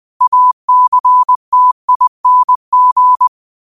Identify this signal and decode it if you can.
Morse code, heard as a keyed tone, about 20 wpm, 1000 Hz. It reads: ACTING
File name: qotdfiles / 20wpm